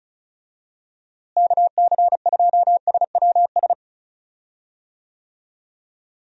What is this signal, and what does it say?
Morse code, heard as a keyed tone, about 34 wpm, 700 Hz. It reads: KC2SWS
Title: USA Call Signs - 6 Characters - Repeated 3x - 35wpm